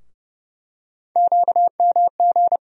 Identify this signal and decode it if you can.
Morse code, heard as a keyed tone, about 30 wpm, 700 Hz. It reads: QMG
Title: USA Call Sign Postfixes - 3 Letters 30wpm